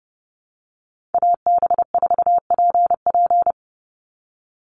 Morse code, heard as a keyed tone, about 30 wpm, 700 Hz. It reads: AB4PP
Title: NA SOTA Chaser Callsigns 30wpm